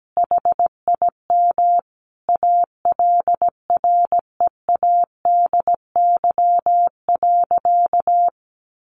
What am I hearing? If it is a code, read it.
Morse code, heard as a keyed tone, about 17 wpm, 700 Hz. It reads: HIM ALREADY.
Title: Sentences from Top 1000 Words 17wpm